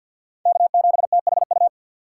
Morse code, heard as a keyed tone, about 49 wpm, 700 Hz. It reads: K7THU
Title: USA Call Signs - 4x Character Spacing - 50wpm